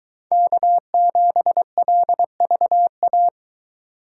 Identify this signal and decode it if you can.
Morse code, heard as a keyed tone, about 22 wpm, 700 Hz. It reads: K7LVA